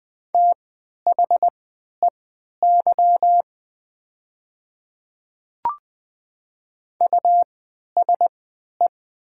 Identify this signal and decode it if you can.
Morse code, heard as a keyed tone, about 20 wpm, spaced as for 11 wpm, 700 Hz. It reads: THEY USE
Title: 3x Character Spacing - Top 100 Words - 20wpm